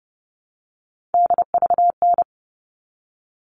Morse code, heard as a keyed tone, about 30 wpm, 700 Hz. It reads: DVN
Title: All Cut Numbers - 3-Digit 30wpm